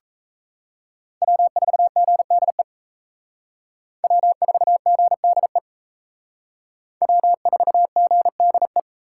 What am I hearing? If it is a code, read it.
Morse code, heard as a keyed tone, about 37 wpm, 700 Hz. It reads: W4GDE W4GDE W4GDE